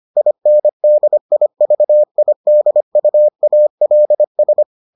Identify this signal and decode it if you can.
Morse code, heard as a keyed tone, about 25 wpm, 600 Hz. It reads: INDIVIDUALS